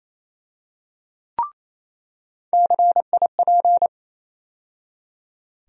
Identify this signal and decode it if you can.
Morse code, heard as a keyed tone, about 28 wpm, 700 Hz. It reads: CIP
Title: USA Call Sign Postfixes - 3 Letters 28wpm